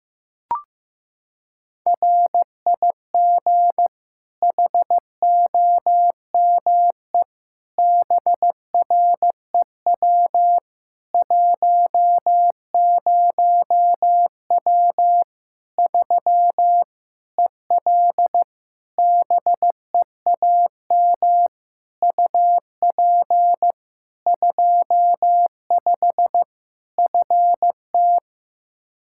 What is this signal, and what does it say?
Morse code, heard as a keyed tone, about 15 wpm, 700 Hz. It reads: RIG HOME BREW 10W 3 EL BEAM UP 25 FT
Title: K1USN SST QSO Practice 15wpm